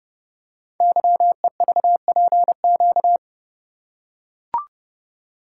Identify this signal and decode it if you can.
Morse code, heard as a keyed tone, about 30 wpm, 700 Hz. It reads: YEVPQ
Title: Code Groups - 5 Letters - 30wpm